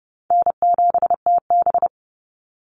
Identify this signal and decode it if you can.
Morse code, heard as a keyed tone, about 30 wpm, 700 Hz. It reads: N7TB
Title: Call Signs - 30wpm